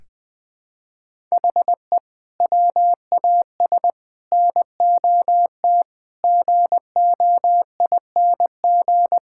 Morse code, heard as a keyed tone, about 20 wpm, 700 Hz. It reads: HE WAS NOT GOING
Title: Build A Phrase - 5-Word Phrases - 20wpm